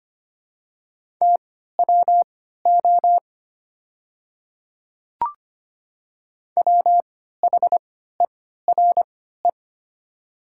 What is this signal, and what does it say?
Morse code, heard as a keyed tone, about 25 wpm, spaced as for 14 wpm, 700 Hz. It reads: TWO WHERE
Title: 3x Character Spacing - Top 200 Words - 25wpm